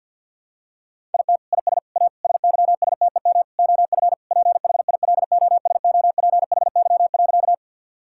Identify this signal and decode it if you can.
Morse code, heard as a keyed tone, about 50 wpm, 700 Hz. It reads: IT IS A SYSTEM OF PHILOSOPHY.
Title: Sentences from Top 1000 Words - Review Part 6 - 50wpm